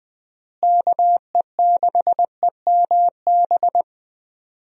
Morse code, heard as a keyed tone, about 21 wpm, 700 Hz. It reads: KE6EMB
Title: USA Call Signs - 6 Characters - Repeated 2x - 20wpm